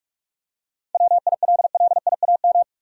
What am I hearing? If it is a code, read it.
Morse code, heard as a keyed tone, about 46 wpm, 700 Hz. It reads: WILLIAM
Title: USA Common Names - 45wpm